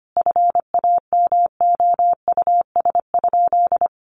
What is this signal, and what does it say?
Morse code, heard as a keyed tone, about 25 wpm, 700 Hz. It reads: FAMOUS?
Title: Sentences from Top 1500 Words 25wpm